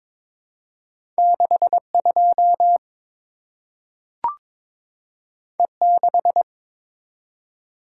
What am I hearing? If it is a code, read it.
Morse code, heard as a keyed tone, about 22 wpm, 700 Hz. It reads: 62 E6
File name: practice / 22wpm / Two Letter-Number Combinations - 22wpm